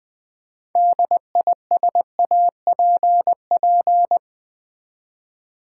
Morse code, heard as a keyed tone, about 20 wpm, 700 Hz. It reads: DISAPP
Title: Build A Word - 3 Letters At A Time - 9-Letter Words - 20wpm